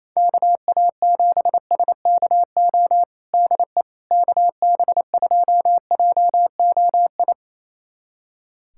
Morse code, heard as a keyed tone, about 27 wpm, 700 Hz. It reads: KA7SKO DE KB2JOI
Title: USA Callsign DE Callsign - 6 Characters - 28wpm